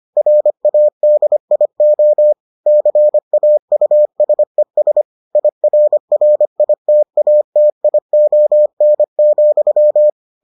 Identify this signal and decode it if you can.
Morse code, heard as a keyed tone, about 25 wpm, 600 Hz. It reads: RADIO CAUSES IRRITATION,